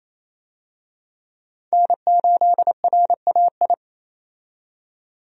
Morse code, heard as a keyed tone, about 28 wpm, 700 Hz. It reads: N8RAI
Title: USA Call Signs - 5 Characters - 28wpm